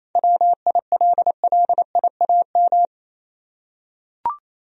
Morse code, heard as a keyed tone, about 28 wpm, 700 Hz. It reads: WILLIAM
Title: USA Common Names - 28wpm